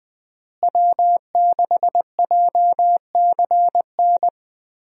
Morse code, heard as a keyed tone, about 20 wpm, 700 Hz. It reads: W6JCN
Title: USA Call Signs - 5 Characters - Repeated 2x - 20wpm